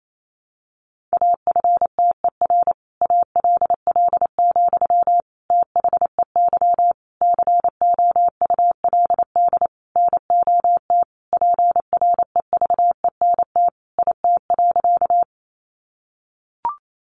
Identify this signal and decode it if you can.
Morse code, heard as a keyed tone, about 28 wpm, 700 Hz. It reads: AFTER ALL, THEY COULD NOT PREVENT IT.
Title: Sentences from Top 1100 Words 28wpm